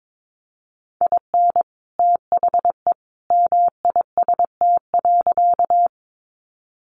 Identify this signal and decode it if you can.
Morse code, heard as a keyed tone, about 22 wpm, 700 Hz. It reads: IN THE MIST.
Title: A Sound Of Thunder 22wpm